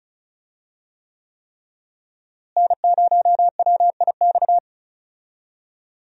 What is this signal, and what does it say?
Morse code, heard as a keyed tone, about 34 wpm, 700 Hz. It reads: N0WIX
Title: USA Call Signs - 5 Characters - Repeated 3x - 35wpm